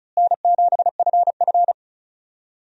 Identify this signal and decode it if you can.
Morse code, heard as a keyed tone, about 36 wpm, 700 Hz. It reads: N7FF
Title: USA Call Signs - 4 Characters - Repeated 2x - 35wpm